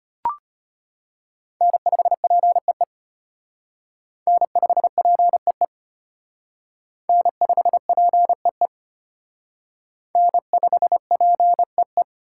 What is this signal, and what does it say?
Morse code, heard as a keyed tone, about 33 wpm, 700 Hz. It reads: N5PEE N5PEE N5PEE N5PEE